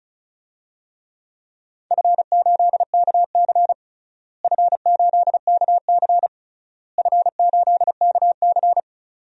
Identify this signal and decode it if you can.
Morse code, heard as a keyed tone, about 34 wpm, 700 Hz. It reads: F8KC F8KC F8KC
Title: DXCC-100 Call Signs - 4 Characters - Repeated 3x - 35wpm